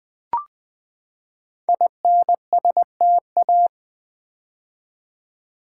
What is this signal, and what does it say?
Morse code, heard as a keyed tone, about 20 wpm, 700 Hz. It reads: INSTA
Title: Build A Word - 5 Letters At A Time - 12-Letter Words - 20wpm